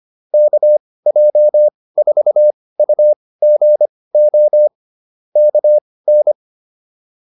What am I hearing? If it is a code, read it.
Morse code, heard as a keyed tone, about 25 wpm, spaced as for 18 wpm, 600 Hz. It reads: KJ4UGO KN